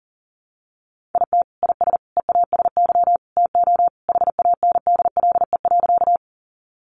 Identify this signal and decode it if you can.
Morse code, heard as a keyed tone, about 40 wpm, 700 Hz. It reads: IT IS EASY TO HANDLE.
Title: Sentences from Top 1100 Words 40wpm